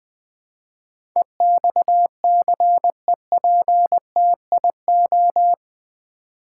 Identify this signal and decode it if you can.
Morse code, heard as a keyed tone, about 20 wpm, 700 Hz. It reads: EXCEPTIO
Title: Build A Word - 4 Letters At A Time - 11-Letter Words - 20wpm